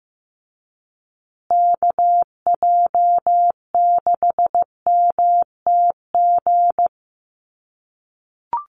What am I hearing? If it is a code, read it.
Morse code, heard as a keyed tone, about 15 wpm, 700 Hz. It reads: KJ6MTG
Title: USA Call Signs Repeated 3x 15wpm